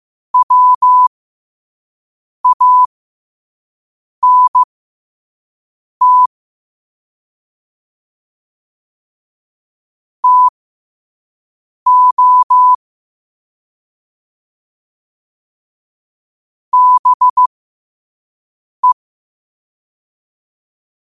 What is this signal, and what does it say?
Morse code, heard as a keyed tone, about 15 wpm, spaced as for 5 wpm, 1000 Hz. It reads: WANT TO BE
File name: qotdfiles / 5wpm